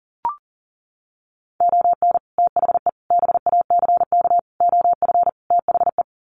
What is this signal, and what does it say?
Morse code, heard as a keyed tone, about 40 wpm, 700 Hz. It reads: ON THE BACK OF THE
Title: Sets of 5 Words 40wpm